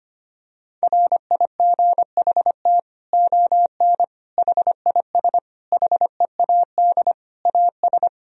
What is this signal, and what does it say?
Morse code, heard as a keyed tone, about 25 wpm, 700 Hz. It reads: RIGHT ON HIS HEAD AS
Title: Sentences from Top 1000 Words - Review Part 2 - 25wpm